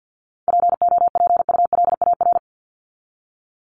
Morse code, heard as a keyed tone, about 50 wpm, 700 Hz. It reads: POPULAR
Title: Top 501-750 Words 50wpm